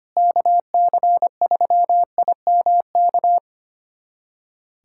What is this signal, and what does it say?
Morse code, heard as a keyed tone, about 25 wpm, 700 Hz. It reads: KC3IMK
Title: USA Callsign DE Callsign - 6 Characters - 25wpm